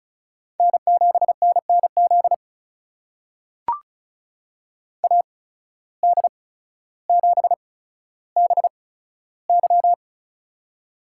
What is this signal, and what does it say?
Morse code, heard as a keyed tone, about 36 wpm, spaced as for 15 wpm, 700 Hz. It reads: N7NNZ AD7BY
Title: USA Call Signs - 5 Characters - 8x Character Spacing - 35wpm